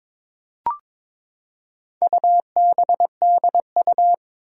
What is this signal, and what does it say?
Morse code, heard as a keyed tone, about 22 wpm, 700 Hz. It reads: UBDU
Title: Four Challenging Letter-Number Combinations 22wpm